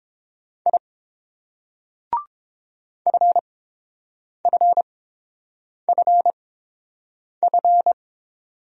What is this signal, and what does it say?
Morse code, heard as a keyed tone, about 30 wpm, spaced as for 7 wpm, 700 Hz. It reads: I FFFF